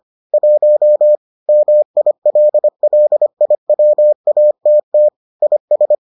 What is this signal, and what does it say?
Morse code, heard as a keyed tone, about 25 wpm, 600 Hz. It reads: 1 MILLIWATT IS